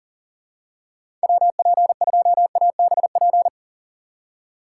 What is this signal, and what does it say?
Morse code, heard as a keyed tone, about 38 wpm, 700 Hz. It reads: WP2ABP
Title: USA Call Signs - 6 Characters - 3x Character Spacing - 40wpm